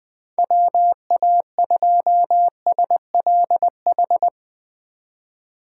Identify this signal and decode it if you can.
Morse code, heard as a keyed tone, about 20 wpm, 700 Hz. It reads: WA2SLH
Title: 2023 POTA Hunter Call Signs - 20wpm